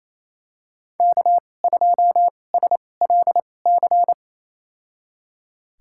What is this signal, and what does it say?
Morse code, heard as a keyed tone, about 28 wpm, spaced as for 20 wpm, 700 Hz. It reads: K2SLC
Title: USA Call Signs - 5 Characters - 2x Character Spacing - 28wpm